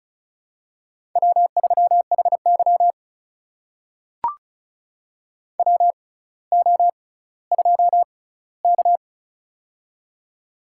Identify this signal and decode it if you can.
Morse code, heard as a keyed tone, about 36 wpm, spaced as for 18 wpm, 700 Hz. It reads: W3HY WO2K
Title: USA Call Signs - 4 Characters - 6x Character Spacing - 35wpm